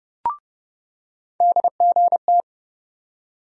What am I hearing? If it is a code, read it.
Morse code, heard as a keyed tone, about 30 wpm, 700 Hz. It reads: DGT
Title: USA Call Sign Postfixes - 3 Letters 30wpm